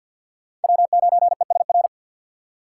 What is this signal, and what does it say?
Morse code, heard as a keyed tone, about 48 wpm, 700 Hz. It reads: W9ESR
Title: USA Call Signs - 5 Characters - 2x Character Spacing - 50wpm